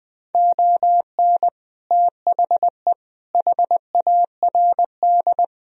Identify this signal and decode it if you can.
Morse code, heard as a keyed tone, about 20 wpm, 700 Hz. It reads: ON THE HARD
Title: Sentences from Top 1000 Words - Review Part 2 - 20wpm